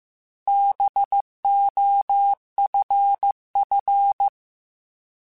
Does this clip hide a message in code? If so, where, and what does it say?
Yes: Morse code from 0.5 to 4.3 s, heard as a keyed tone, about 15 wpm, 800 Hz. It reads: BOFF